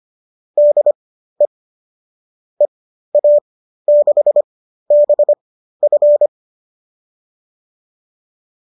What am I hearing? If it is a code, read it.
Morse code, heard as a keyed tone, about 25 wpm, spaced as for 13 wpm, 590 Hz. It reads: DE EA6BF
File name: qso214 13